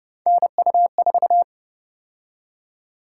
Morse code, heard as a keyed tone, about 29 wpm, 700 Hz. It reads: NU4